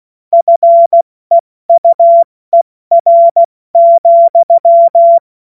Morse code, heard as a keyed tone, about 16 wpm, 680 Hz. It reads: FEUER,
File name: Morsekurs / Witze / Witz 14. Juni 2022 70 F80